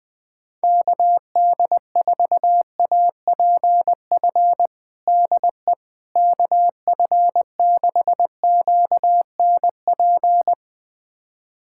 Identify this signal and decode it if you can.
Morse code, heard as a keyed tone, about 21 wpm, 700 Hz. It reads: KD4APF DE KF6QNP
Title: USA Callsign DE Callsign - 6 Characters - 20wpm